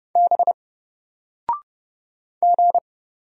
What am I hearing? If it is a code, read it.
Morse code, heard as a keyed tone, about 31 wpm, 700 Hz. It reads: B G
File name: Single-Letter-Number-30wpm